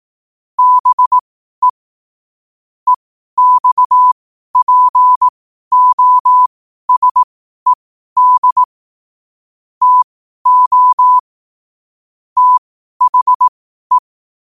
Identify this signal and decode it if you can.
Morse code, heard as a keyed tone, about 18 wpm, spaced as for 12 wpm, 1000 Hz. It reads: BE EXPOSED TO THE
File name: qotdfiles / 12wpm